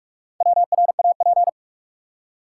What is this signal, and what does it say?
Morse code, heard as a keyed tone, about 46 wpm, 700 Hz. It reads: WRAP
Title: Four Letter Words - 45wpm